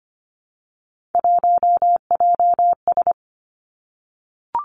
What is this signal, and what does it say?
Morse code, heard as a keyed tone, about 25 wpm, 700 Hz. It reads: 1JS
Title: Three Challenging Letter-Number Combinations 25wpm